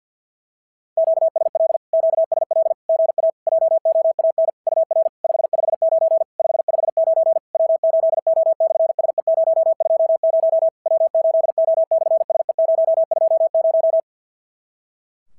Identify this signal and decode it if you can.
Morse code, heard as a keyed tone, about 54 wpm, 650 Hz. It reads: QSL QSL GA JOAN UR 559 559 W8O/SE020 W8O/SE020
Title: SOTA QSO Practice 50wpm